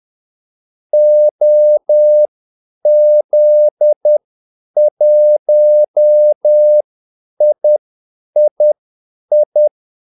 Morse code, heard as a keyed tone, about 10 wpm, spaced as for 8 wpm, 600 Hz. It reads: OZ1III
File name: Callsigns-10-8wpm